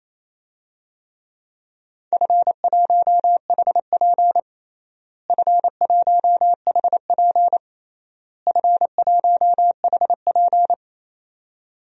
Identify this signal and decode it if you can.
Morse code, heard as a keyed tone, about 28 wpm, 700 Hz. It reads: F1HP F1HP F1HP
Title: DXCC-100 Call Signs - 4 Characters - Repeated 3x - 28wpm